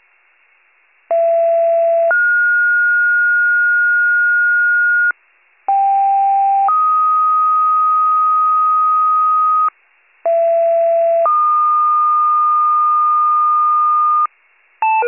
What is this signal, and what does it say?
Two-tone page, heard as a one-second tone, then a three-second tone, about 670 then 1430 Hz, about 790 then 1250 Hz, about 670 then 1220 Hz. It is heard from 1.1 to 14.3 s.